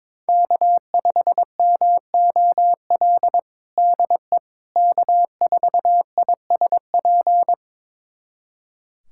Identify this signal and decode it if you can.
Morse code, heard as a keyed tone, about 22 wpm, 700 Hz. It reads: K5MOL DE K4ISP
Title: USA Callsign DE Callsign - 5 Characters - 22wpm